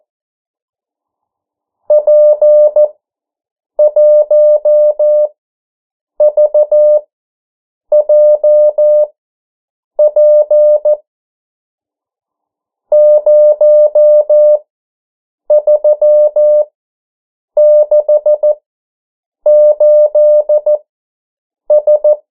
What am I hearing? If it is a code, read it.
Morse code, heard as a keyed tone, about 14 wpm, spaced as for 7 wpm, 600 Hz. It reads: P1VJP 0368S
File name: morse mp3 / 09 urok 8